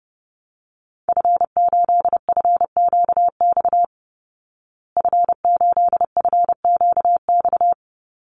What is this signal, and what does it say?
Morse code, heard as a keyed tone, about 30 wpm, 700 Hz. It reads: F8FQX F8FQX
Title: Call Signs Repeated 3x - 30wpm